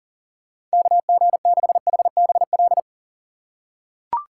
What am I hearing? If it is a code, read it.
Morse code, heard as a keyed tone, about 37 wpm, 700 Hz. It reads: KG6HBL
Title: USA Call Signs - 6 Characters - 4x Character Spacing - 40wpm